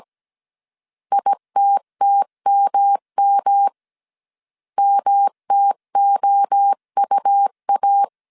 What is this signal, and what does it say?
Morse code, heard as a keyed tone, about 17 wpm, 800 Hz. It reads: ITTMM MTOUA
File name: CW / DET / 1 3 1500C S C 16wpm 2148